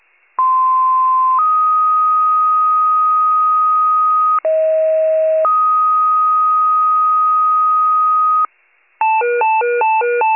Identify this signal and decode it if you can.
Two-tone page, heard as a one-second tone, then a three-second tone, about 1040 then 1250 Hz, about 640 then 1250 Hz. It is heard from 0.4 to 8.5 s.